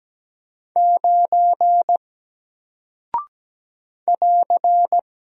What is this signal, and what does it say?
Morse code, heard as a keyed tone, about 17 wpm, 700 Hz. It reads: 9 <AR>
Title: Single Character (Letter:Number:Punctuation:Prosign) Alt 17wpm